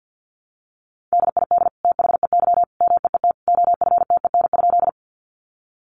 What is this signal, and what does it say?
Morse code, heard as a keyed tone, about 50 wpm, 700 Hz. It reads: DID THEY MEET OFTEN?
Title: Sentences from Top 1100 Words 50wpm